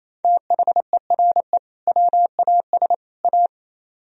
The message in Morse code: THERE WAS A

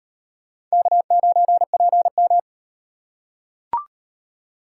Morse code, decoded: K9PM

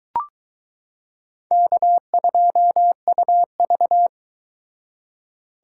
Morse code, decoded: K2UV

